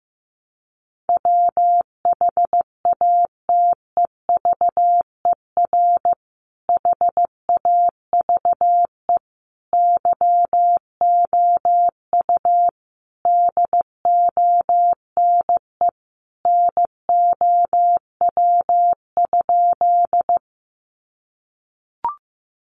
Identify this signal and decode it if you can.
Morse code, heard as a keyed tone, about 15 wpm, 700 Hz. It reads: WHATEVER HAVE YOU DONE NOW?
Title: Sentences from Top 1100 Words 15wpm